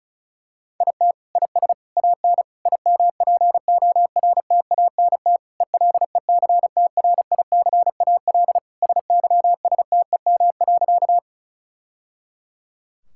Morse code, heard as a keyed tone, about 36 wpm, 700 Hz. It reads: IT IS AN IMPORTANT ELECTRICAL SYSTEM.
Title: Sentences from Top 1000 Words - Review Part 6 - 35wpm